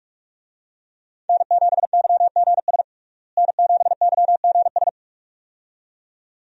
Morse code, heard as a keyed tone, about 42 wpm, 700 Hz. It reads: N7YGS N7YGS
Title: USA Call Signs - 5 Characters - Repeated 2x - 45wpm